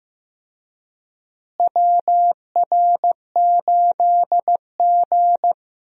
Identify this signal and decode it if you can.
Morse code, heard as a keyed tone, about 15 wpm, 700 Hz. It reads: WR8G
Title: USA Call Signs - 4 Characters - Repeated 2x - 15wpm